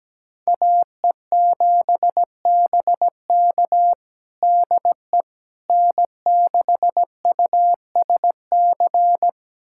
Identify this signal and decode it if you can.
Morse code, heard as a keyed tone, about 17 wpm, 700 Hz. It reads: AE7BK DE N6USC
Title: USA Callsign DE Callsign - 5 Characters - 17wpm